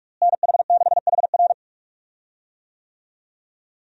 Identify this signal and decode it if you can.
Morse code, heard as a keyed tone, about 46 wpm, 700 Hz. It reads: NH6HR
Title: USA Call Signs - 5 Characters - Repeated 3x - 45wpm